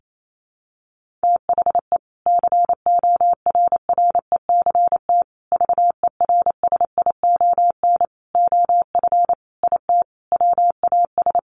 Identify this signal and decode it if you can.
Morse code, heard as a keyed tone, about 28 wpm, 700 Hz. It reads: THE CORRECT VERSION OF IT WAS
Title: Sentences from Top 1400 Words 28wpm